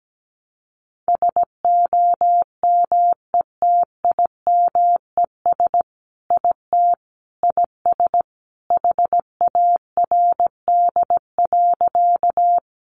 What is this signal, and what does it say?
Morse code, heard as a keyed tone, about 17 wpm, 700 Hz. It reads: SOMETIMES IT IS HARD.
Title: Sentences from Top 600 Words 17wpm